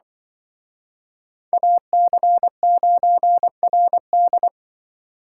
Morse code, heard as a keyed tone, about 25 wpm, 700 Hz. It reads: AC9RD